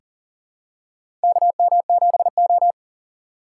Morse code, heard as a keyed tone, about 40 wpm, 700 Hz. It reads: KM7O
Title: USA Call Signs - 4 Characters - 4x Character Spacing - 40wpm